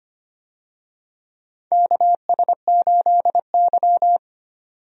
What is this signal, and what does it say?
Morse code, heard as a keyed tone, about 25 wpm, 700 Hz. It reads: KS8Y
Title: USA Call Signs - 4 Characters - 25wpm